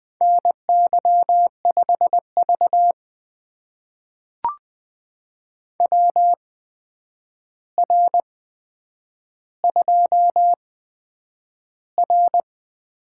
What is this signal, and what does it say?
Morse code, heard as a keyed tone, about 20 wpm, spaced as for 9 wpm, 700 Hz. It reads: NY5V WR2R
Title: USA Call Signs - 4 Characters - 8x Character Spacing - 20wpm